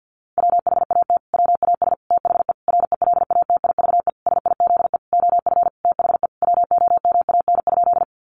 Morse code, heard as a keyed tone, about 50 wpm, 700 Hz. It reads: WHAT WAS THE RELATIVE SIZE OF THE WOMAN?
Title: Sentences from Top 1400 Words 50wpm